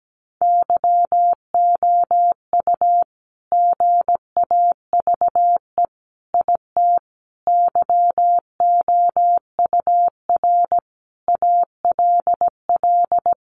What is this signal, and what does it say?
Morse code, heard as a keyed tone, about 17 wpm, 700 Hz. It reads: YOU GAVE IT YOUR ALL
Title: Sentences from Top 1500 Words 17wpm